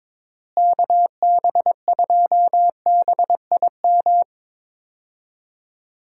Morse code, heard as a keyed tone, about 22 wpm, 700 Hz. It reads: KB2BIM
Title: USA Call Signs - 6 Characters - Repeated 2x - 22wpm